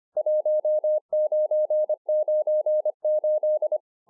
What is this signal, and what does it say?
Morse code, heard as a keyed tone, about 25 wpm, 610 Hz. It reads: 1998